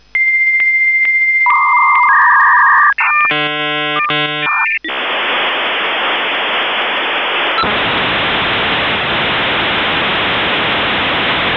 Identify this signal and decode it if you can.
Dial-up modem handshake.
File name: modem / v90usr1